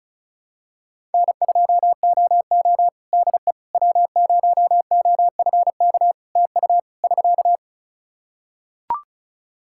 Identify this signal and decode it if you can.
Morse code, heard as a keyed tone, about 36 wpm, 700 Hz. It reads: N2OO DE W0OFK TU <SK>
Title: K1USN SST QSO Practice 35wpm